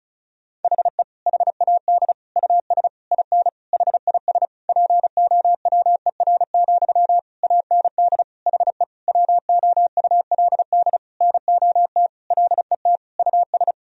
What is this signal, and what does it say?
Morse code, heard as a keyed tone, about 34 wpm, 700 Hz. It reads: HE HAD US IN HIS POWER, AND HE WOULD NOT LET US
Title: Sentences from Top 1000 Words - Review Part 2 - 35wpm